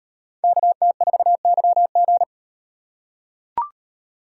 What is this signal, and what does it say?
Morse code, heard as a keyed tone, about 35 wpm, 700 Hz. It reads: KT4YG